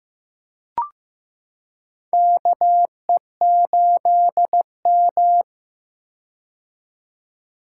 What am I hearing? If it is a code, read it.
Morse code, heard as a keyed tone, about 16 wpm, 700 Hz. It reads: KE8M
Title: K1USN SST QSO Practice - 15wpm